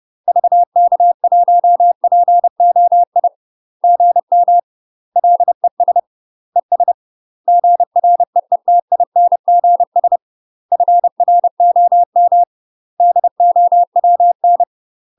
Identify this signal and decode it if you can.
Morse code, heard as a keyed tone, about 30 wpm, 700 Hz. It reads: VK1POI GM LES ES GREETINGS FROM DOWN
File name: UK MASTER QSOs TRAINING / 30WPM / UK MASTER QSOs 0003